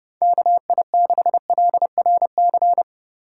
Morse code, heard as a keyed tone, about 36 wpm, 700 Hz. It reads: KI6LRC
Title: Speed Racing - USA Call Signs 30wpm